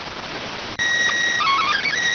SSTV (slow-scan television), PD 290.